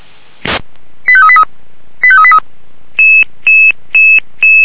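Five-tone selective call (ZVEI), heard as quick runs of five tones, tones of about 70 ms. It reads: 85383 85383
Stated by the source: ZVEI five-tone call 85383 received by SDR.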